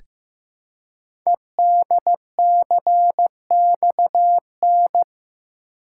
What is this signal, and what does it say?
Morse code, heard as a keyed tone, about 15 wpm, 700 Hz. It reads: EDCXN
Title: Code Groups - 5 Letters - 15wpm